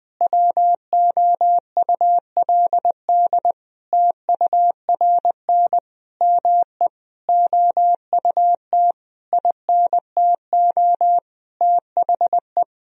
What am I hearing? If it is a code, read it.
Morse code, heard as a keyed tone, about 20 wpm, 700 Hz. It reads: WOULD TURN ME OUT INTO THE
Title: Sentences from Top 1000 Words - Review Part 4 - 20wpm